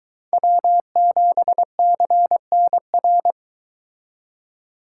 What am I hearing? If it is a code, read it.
Morse code, heard as a keyed tone, about 22 wpm, 700 Hz. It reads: W7CNR